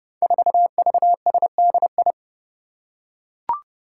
Morse code, heard as a keyed tone, about 30 wpm, 700 Hz. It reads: 4VSDI
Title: Five Challenging Letter-Number Combinations 30wpm